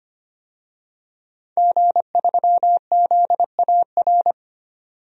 Morse code, heard as a keyed tone, about 25 wpm, 700 Hz. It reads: G3ZAR
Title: DXCC-100 Call Signs - 5 Characters - 25wpm